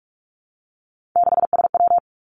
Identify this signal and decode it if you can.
Morse code, heard as a keyed tone, about 46 wpm, 700 Hz. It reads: 6SW
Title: Three Challenging Letter-Number Combinations 45wpm